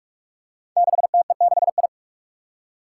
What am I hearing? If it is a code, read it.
Morse code, heard as a keyed tone, about 44 wpm, 700 Hz. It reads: 6TE6I
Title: Five Challenging Letter-Number Combinations 45wpm